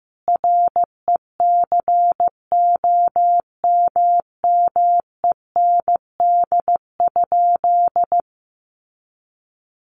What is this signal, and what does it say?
Morse code, heard as a keyed tone, about 15 wpm, 700 Hz. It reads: RECOMMEND?
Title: Sentences from Top 1200 Words 15wpm